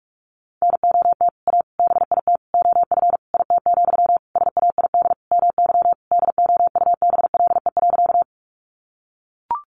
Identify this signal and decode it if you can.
Morse code, heard as a keyed tone, about 44 wpm, 700 Hz. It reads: NOT A BIT OF IT, SAID MY DOUBLE.
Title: Sentences from Top 1400 Words 45wpm